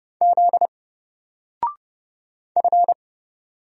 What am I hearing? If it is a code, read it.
Morse code, heard as a keyed tone, about 30 wpm, 700 Hz. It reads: Z F
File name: Single-Letter-Number-30wpm